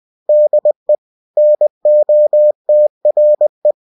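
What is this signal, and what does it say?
Morse code, heard as a keyed tone, about 20 wpm, 600 Hz. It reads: DE NOTRE
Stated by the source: Bulletin sent in CW.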